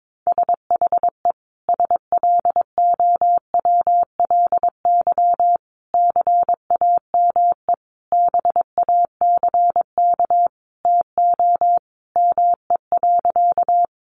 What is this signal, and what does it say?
Morse code, heard as a keyed tone, about 22 wpm, 700 Hz. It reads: SHE SLOWLY CAME BACK TO ME.
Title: Sentences from Top 1100 Words 22wpm